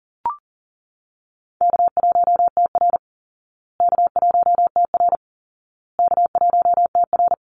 Morse code, heard as a keyed tone, about 40 wpm, 700 Hz. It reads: K1TR K1TR K1TR
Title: Easy Call Signs Repeated 3x - 40wpm